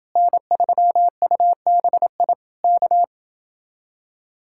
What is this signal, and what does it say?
Morse code, heard as a keyed tone, about 31 wpm, 700 Hz. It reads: N3UBI K